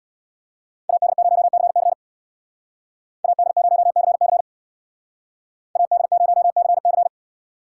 Morse code, heard as a keyed tone, about 64 wpm, 700 Hz. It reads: AD0CC AD0CC AD0CC